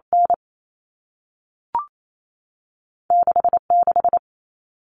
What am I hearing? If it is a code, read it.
Morse code, heard as a keyed tone, about 28 wpm, spaced as for 3 wpm, 700 Hz. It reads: N6 6
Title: Common Cut Numbers - 2-Digit 28wpm